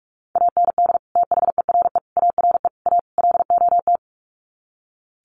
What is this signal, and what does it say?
Morse code, heard as a keyed tone, about 44 wpm, 700 Hz. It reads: AND THERE ARE A LOT
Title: Sets of 5 Words Encore2 45wpm